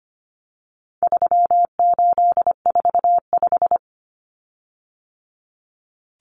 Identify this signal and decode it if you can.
Morse code, heard as a keyed tone, about 25 wpm, 700 Hz. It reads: 3845
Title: Common Cut Numbers - 4-Digit 25wpm